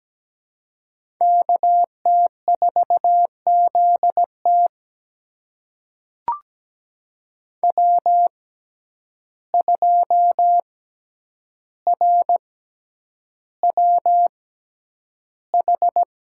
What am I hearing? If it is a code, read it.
Morse code, heard as a keyed tone, about 17 wpm, spaced as for 9 wpm, 700 Hz. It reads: KT4ZT W2RWH
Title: USA Call Signs - 5 Characters - 6x Character Spacing - 17wpm